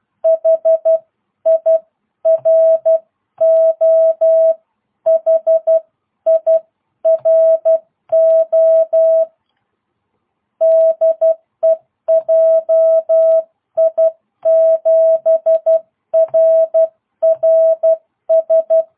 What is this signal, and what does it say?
Morse code, heard as a keyed tone, about 12 wpm, 660 Hz. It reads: HIRO HIRO DE JI7RRS